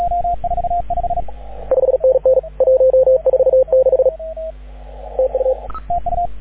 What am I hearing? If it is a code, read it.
Morse code, heard as a keyed tone, about 37 wpm, 520 Hz, with radio noise. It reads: 5NN 146 TU
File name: AUDIO2019 / HG6N CW SSB MIX